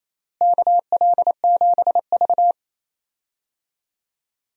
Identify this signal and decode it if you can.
Morse code, heard as a keyed tone, about 28 wpm, 700 Hz. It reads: KL7V
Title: DXCC-100 Call Signs - 4 Characters - Repeated 2x - 28wpm